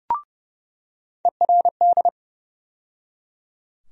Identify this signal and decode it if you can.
Morse code, heard as a keyed tone, about 30 wpm, 700 Hz. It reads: ERD
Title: USA Call Sign Postfixes - 3 Letters 30wpm